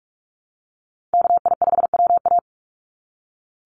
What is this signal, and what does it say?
Morse code, heard as a keyed tone, about 44 wpm, 700 Hz. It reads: KI5WA
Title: NA SOTA Chaser Callsigns 45wpm